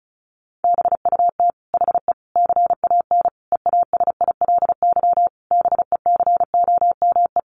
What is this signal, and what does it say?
Morse code, heard as a keyed tone, about 34 wpm, 700 Hz. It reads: BUT HE CAN EASILY BECOME
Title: Sentences from Top 900 Words 35wpm